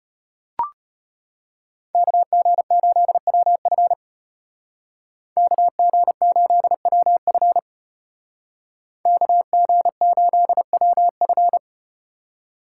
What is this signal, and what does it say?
Morse code, heard as a keyed tone, about 34 wpm, 700 Hz. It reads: KG8WF KG8WF KG8WF